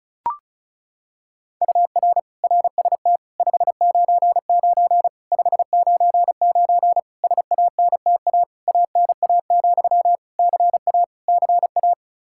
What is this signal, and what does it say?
Morse code, heard as a keyed tone, about 36 wpm, 700 Hz. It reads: UR RST 599 599 SANTA ANA, CA CA
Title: Practice QSO - Exchange 1 - Beginning - 35wpm